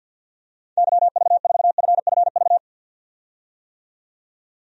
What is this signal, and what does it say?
Morse code, heard as a keyed tone, about 51 wpm, 700 Hz. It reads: YV4FFV
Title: DXCC-100 Call Signs - 6 Characters - 50wpm